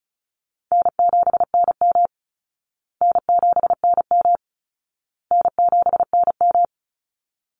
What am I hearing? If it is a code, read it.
Morse code, heard as a keyed tone, about 34 wpm, 700 Hz. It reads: N7NM N7NM N7NM
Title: Call Signs Repeated 2x - 35wpm